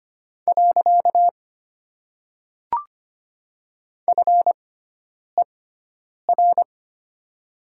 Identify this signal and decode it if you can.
Morse code, heard as a keyed tone, about 25 wpm, 700 Hz. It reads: . F E R
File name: Top 90 QSO Elements - 6x Character Spacing - 25wpm